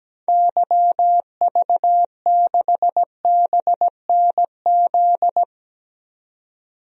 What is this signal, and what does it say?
Morse code, heard as a keyed tone, about 17 wpm, 700 Hz. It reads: YV6BNZ
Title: DXCC-100 Call Signs - 6 Characters - 17wpm